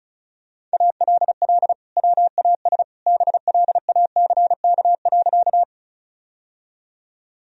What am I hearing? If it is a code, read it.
Morse code, heard as a keyed tone, about 34 wpm, 700 Hz. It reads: ALL WAS BLACK.
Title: Sentences from Top 1000 Words - Review Part 2 - 35wpm